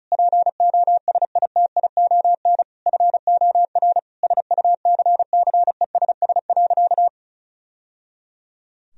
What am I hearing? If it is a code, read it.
Morse code, heard as a keyed tone, about 36 wpm, 700 Hz. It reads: POSITION FOR SUCCESS.
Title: Sentences from Top 1000 Words - Review Part 4 - 35wpm